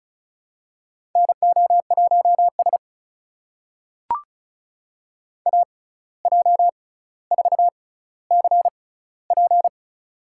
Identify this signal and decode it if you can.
Morse code, heard as a keyed tone, about 36 wpm, spaced as for 17 wpm, 700 Hz. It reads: NO1S AJ4CP
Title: USA Call Signs - 6x Character Spacing - 35wpm